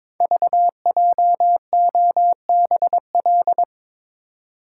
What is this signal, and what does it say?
Morse code, heard as a keyed tone, about 21 wpm, 700 Hz. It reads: VJOBL
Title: Code Groups - 5 Letters - 22wpm